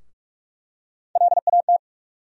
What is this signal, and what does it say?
Morse code, heard as a keyed tone, about 46 wpm, 700 Hz. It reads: LAT
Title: USA Call Sign Postfixes - 3 Letters 45wpm